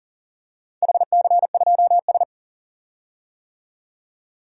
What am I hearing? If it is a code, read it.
Morse code, heard as a keyed tone, about 37 wpm, 700 Hz. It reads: HC2S
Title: DXCC-100 Call Signs - 4 Characters - 40wpm